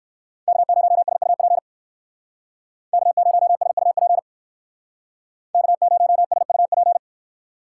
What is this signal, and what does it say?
Morse code, heard as a keyed tone, about 57 wpm, 700 Hz. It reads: K0SUP K0SUP K0SUP